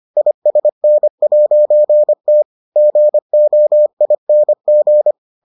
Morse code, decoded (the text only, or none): ISN'T GOING